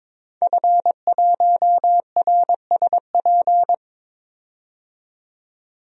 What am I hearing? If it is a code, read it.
Morse code, heard as a keyed tone, about 22 wpm, 700 Hz. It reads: F1RSP
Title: DXCC-100 Call Signs - 5 Characters - 22wpm